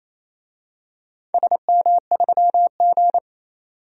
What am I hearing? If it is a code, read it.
Morse code, heard as a keyed tone, about 28 wpm, 700 Hz. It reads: SM3G
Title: DXCC-100 Call Signs - 4 Characters - 28wpm